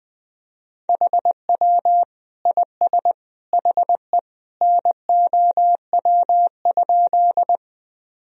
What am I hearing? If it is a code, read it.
Morse code, heard as a keyed tone, about 20 wpm, 700 Hz. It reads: HW IS HE NOW?
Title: Sentences from Top 100 Words and Abbreviations 20wpm